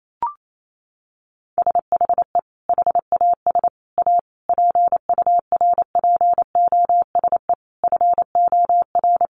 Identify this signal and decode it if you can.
Morse code, heard as a keyed tone, about 28 wpm, 700 Hz. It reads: SHE HAS A PURPOSE FOR
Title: Sentences from Top 800 Words 28wpm